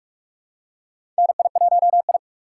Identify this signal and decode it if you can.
Morse code, heard as a keyed tone, about 44 wpm, 700 Hz. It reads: NI1I
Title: USA Call Signs - 4 Characters - 6x Character Spacing - 45wpm